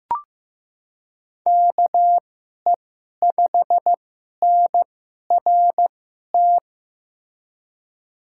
Morse code, heard as a keyed tone, about 15 wpm, spaced as for 11 wpm, 700 Hz. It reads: KE5NRT
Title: USA Call Signs - 6 Characters - 2x Character Spacing - 15wpm